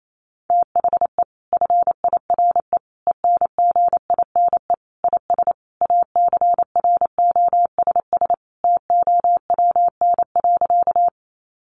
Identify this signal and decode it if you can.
Morse code, heard as a keyed tone, about 28 wpm, 700 Hz. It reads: THE FIRE ENGINE IS ACROSS TOWN.
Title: Sentences from Top 1500 Words 28wpm